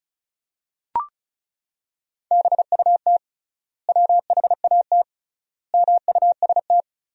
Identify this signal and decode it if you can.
Morse code, heard as a keyed tone, about 36 wpm, 700 Hz. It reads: BUT WHAT MUST
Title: 3x Word-Spacing - Sentences from Top 500 Words - 35wpm